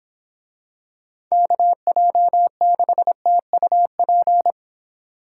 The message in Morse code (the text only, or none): KJ6TUP